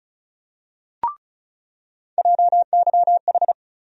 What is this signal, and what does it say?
Morse code, heard as a keyed tone, about 36 wpm, 700 Hz. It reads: JYH
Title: USA Call Sign Postfixes - 3 Letters 35wpm